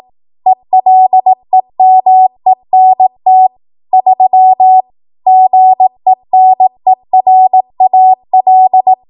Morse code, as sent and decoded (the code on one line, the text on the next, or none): . .-.. -- . -. - / ...-- / --. -. .-. .- .-..
ELEMENT 3 GENERAL